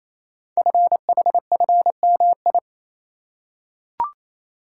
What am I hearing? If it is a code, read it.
Morse code, heard as a keyed tone, about 27 wpm, 700 Hz. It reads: FHFMI
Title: Code Groups - 5 Letter:Numbers - 28wpm